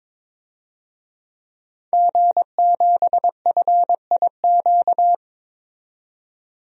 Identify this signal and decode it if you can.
Morse code, heard as a keyed tone, about 22 wpm, 700 Hz. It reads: G7FIQ
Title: DXCC-100 Call Signs - 5 Characters - 22wpm